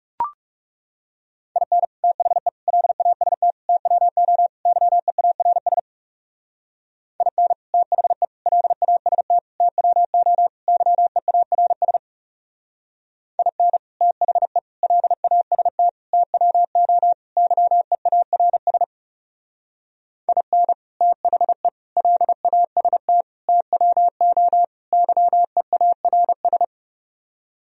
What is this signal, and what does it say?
Morse code, heard as a keyed tone, about 34 wpm, 700 Hz. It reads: IN THE LAST TWO YEARS IN THE LAST TWO YEARS IN THE LAST TWO YEARS IN THE LAST TWO YEARS
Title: Speed-Racing - Sets of 5 Words 30wpm